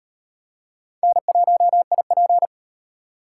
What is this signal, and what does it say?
Morse code, heard as a keyed tone, about 38 wpm, 700 Hz. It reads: N1IP